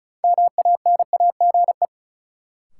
Morse code, heard as a keyed tone, about 36 wpm, 700 Hz. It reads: MANAGE
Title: Sentences from Top 1400 Words - New Word Drill Alt - 35wpm